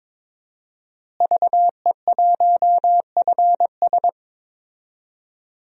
Morse code, heard as a keyed tone, about 22 wpm, 700 Hz. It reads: VE1FS
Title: DXCC-100 Call Signs - 5 Characters - Repeated 2x - 22wpm